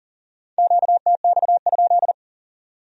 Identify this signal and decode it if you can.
Morse code, heard as a keyed tone, about 50 wpm, 700 Hz. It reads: QTX?
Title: Speed-Racing Q-Codes - 40wpm